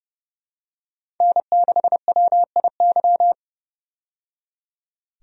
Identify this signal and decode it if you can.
Morse code, heard as a keyed tone, about 30 wpm, 700 Hz. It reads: N6WIY
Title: USA Call Signs - 5 Characters - 30wpm